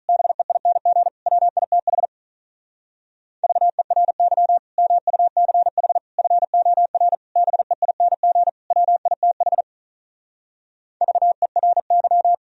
Morse code, heard as a keyed tone, about 45 wpm, 700 Hz. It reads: BEING WITH VERY MUCH FOR BEING WITH VERY